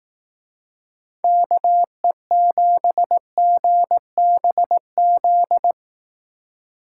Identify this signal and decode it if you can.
Morse code, heard as a keyed tone, about 18 wpm, 700 Hz. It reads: KE7GBZ
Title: Speed Racing - USA 2x3 Call Signs 15wpm